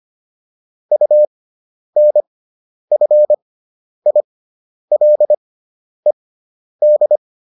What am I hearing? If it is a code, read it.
Morse code, heard as a keyed tone, about 25 wpm, spaced as for 10 wpm, 600 Hz. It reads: UNFILED